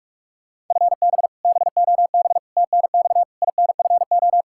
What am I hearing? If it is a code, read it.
Morse code, heard as a keyed tone, about 41 wpm, 700 Hz. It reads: FB BOB TNX INFO
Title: Practice QSO - Exchange 3 - More Info - 45wpm